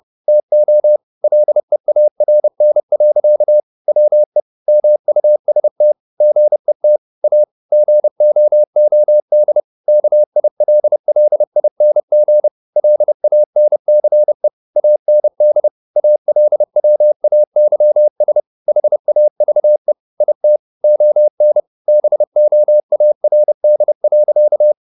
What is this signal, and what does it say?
Morse code, heard as a keyed tone, about 30 wpm, 600 Hz. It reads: TO LEARN. WE MUST GET A GOOD KILLING LANCE AND ALWAYS HAVE IT ON BOARD.